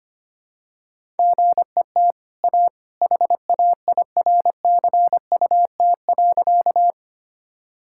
Morse code, heard as a keyed tone, about 25 wpm, 700 Hz. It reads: GET A HAIRCUT.
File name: Easy Sentences with Hints - 25wpm